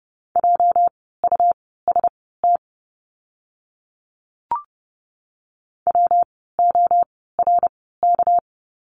Morse code, heard as a keyed tone, about 30 wpm, spaced as for 17 wpm, 700 Hz. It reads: JUST WORK
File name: course / 30:10wpm / 237 Lesson 38f - Letter q - Words 30wpm